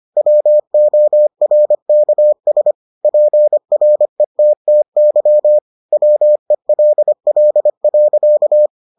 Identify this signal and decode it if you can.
Morse code, heard as a keyed tone, about 25 wpm, 600 Hz. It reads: WORKS PRETTY WELL.